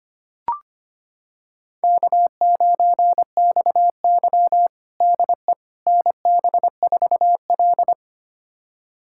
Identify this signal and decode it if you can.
Morse code, heard as a keyed tone, about 25 wpm, 700 Hz. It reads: K9XY DE NB4L
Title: USA Callsign DE Callsign - 4 Characters - 25wpm